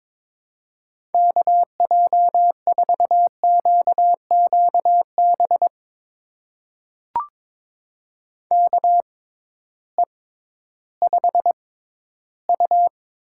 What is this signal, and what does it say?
Morse code, heard as a keyed tone, about 21 wpm, spaced as for 13 wpm, 700 Hz. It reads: KJ4QQB KE5U
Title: USA Call Signs - 6 Characters - 6x Character Spacing - 22wpm